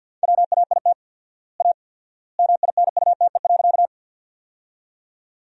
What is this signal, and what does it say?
Morse code, heard as a keyed tone, about 50 wpm, 700 Hz. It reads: WAIT A MINUTE.
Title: 4x Word-Spacing - Sentences from Top 500 Words - 50wpm